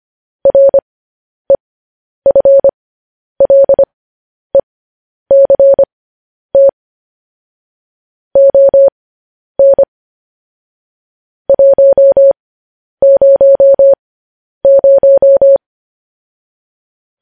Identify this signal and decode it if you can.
Morse code, heard as a keyed tone, about 25 wpm, spaced as for 10 wpm, 550 Hz. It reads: REFLECT ON 100
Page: news headlines in morse code at 10 wpm